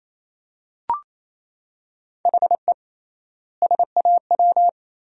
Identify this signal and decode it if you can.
Morse code, heard as a keyed tone, about 28 wpm, 700 Hz. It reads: HE SAW
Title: 3x Word-Spacing - Sentences from Top 500 Words - 28wpm